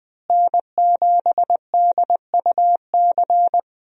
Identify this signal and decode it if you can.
Morse code, heard as a keyed tone, about 20 wpm, 700 Hz. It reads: N7DUC
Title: USA Call Signs - 5 Characters - Repeated 3x - 20wpm